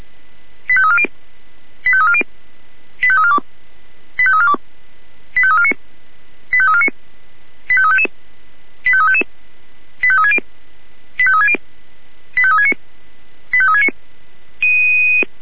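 Five-tone selective call (ZVEI), heard as quick runs of five tones, tones of about 70 ms. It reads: 85340 85340 85342 85342 85349 85349 85355 85355 85360 85360 85369 85369
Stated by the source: ZVEI five-tone selective call received on SDR.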